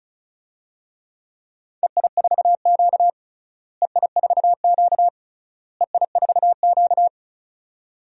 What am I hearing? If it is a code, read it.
Morse code, heard as a keyed tone, about 34 wpm, 700 Hz. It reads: EI4Q EI4Q EI4Q
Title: DXCC-100 Call Signs - 4 Characters - Repeated 3x - 35wpm